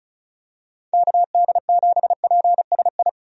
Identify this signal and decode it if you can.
Morse code, heard as a keyed tone, about 36 wpm, 700 Hz. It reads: KD7PSI
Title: USA Call Signs - 6 Characters - 35wpm